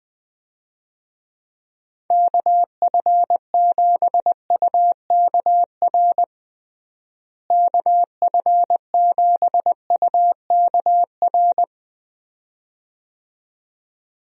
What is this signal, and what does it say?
Morse code, heard as a keyed tone, about 20 wpm, 700 Hz. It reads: KF7UKR KF7UKR
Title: USA Call Signs - 6 Characters - Repeated 2x - 20wpm